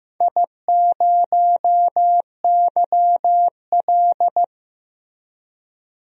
Morse code, heard as a keyed tone, about 15 wpm, 700 Hz. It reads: I0YL
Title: DXCC-100 Call Signs - 4 Characters - Repeated 3x - 15wpm